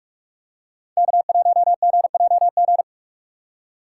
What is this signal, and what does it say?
Morse code, heard as a keyed tone, about 46 wpm, 700 Hz. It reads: K1GJG
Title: USA Call Signs - 5 Characters - 3x Character Spacing - 45wpm